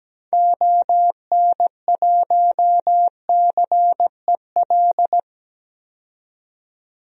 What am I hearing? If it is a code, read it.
Morse code, heard as a keyed tone, about 17 wpm, 700 Hz. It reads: ON1CEL
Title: DXCC-100 Call Signs - 6 Characters - 17wpm